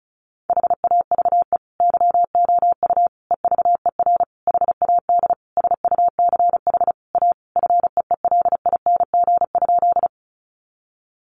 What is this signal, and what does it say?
Morse code, heard as a keyed tone, about 34 wpm, 700 Hz. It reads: HAVE YOU EVER HAD SUCH A FEELING?
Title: Sentences from Top 1200 Words 35wpm